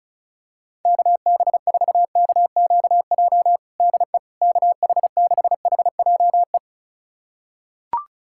Morse code, heard as a keyed tone, about 36 wpm, 700 Hz. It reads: KB4KQJ DE KH6HJE
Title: USA Callsign DE Callsign 35wpm